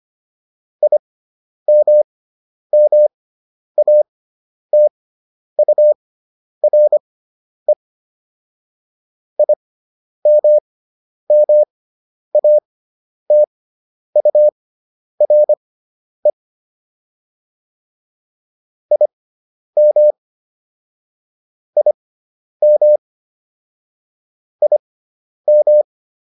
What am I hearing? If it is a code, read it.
Morse code, heard as a keyed tone, about 25 wpm, spaced as for 10 wpm, 600 Hz. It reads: IMMATURE IMMATURE IM IM IM